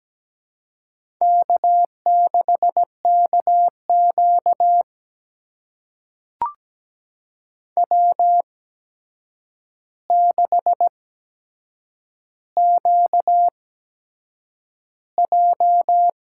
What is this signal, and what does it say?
Morse code, heard as a keyed tone, about 17 wpm, spaced as for 7 wpm, 700 Hz. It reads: K6KQ W6QJ
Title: USA Call Signs - 4 Characters - 8x Character Spacing - 17wpm